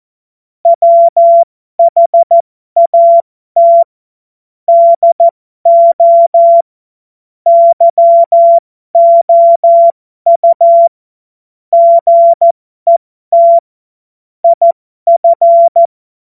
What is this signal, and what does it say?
Morse code, heard as a keyed tone, about 14 wpm, spaced as for 12 wpm, 680 Hz. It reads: WHAT DO YOU GET IF
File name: Morsekurs / Witze